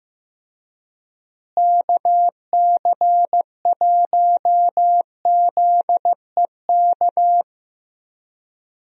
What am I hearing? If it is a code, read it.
Morse code, heard as a keyed tone, about 15 wpm, 700 Hz. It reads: KC1ZEK
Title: USA Call Signs - 6 Characters - Repeated 3x - 15wpm